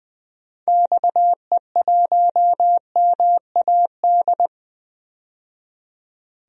Morse code, heard as a keyed tone, about 20 wpm, 700 Hz. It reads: XE1MAD
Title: DXCC-100 Call Signs - 6 Characters - 20wpm